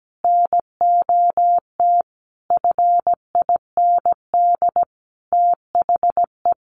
Morse code, heard as a keyed tone, about 17 wpm, 700 Hz. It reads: NOT FIND THE
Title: Sentences from Top 1400 Words 17wpm